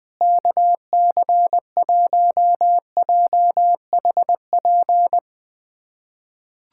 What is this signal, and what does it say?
Morse code, heard as a keyed tone, about 20 wpm, 700 Hz. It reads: KC1JHP
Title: USA Callsign DE Callsign 20wpm